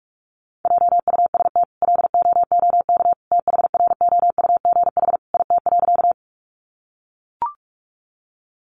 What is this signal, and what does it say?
Morse code, heard as a keyed tone, about 44 wpm, 700 Hz. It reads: JUST LOOK THROUGH IT.
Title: Sentences from Top 200 Words 45wpm